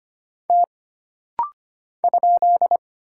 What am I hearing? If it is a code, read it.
Morse code, heard as a keyed tone, about 25 wpm, 700 Hz. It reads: T ?
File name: Single Character (Letter:Number:Punctuation:Prosign) - Rapid-Fire - 25wpm